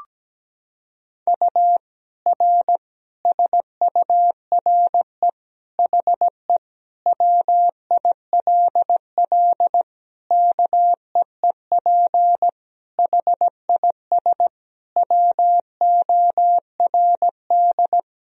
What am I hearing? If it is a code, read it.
Morse code, heard as a keyed tone, about 17 wpm, 700 Hz. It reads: U R SURE HE WILL KEEP HIS WORD
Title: Sentences from Top 300 Words and Abbreviations 17wpm